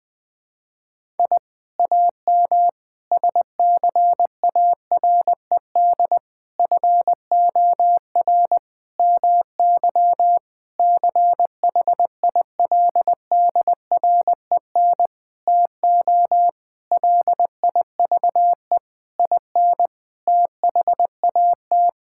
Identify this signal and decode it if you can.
Morse code, heard as a keyed tone, about 21 wpm, 700 Hz. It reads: I AM SCARED FOR MY CHILDREN TO LIVE IN THAT
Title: Sentences from Top 1000 Words - Review Part 5 - 20wpm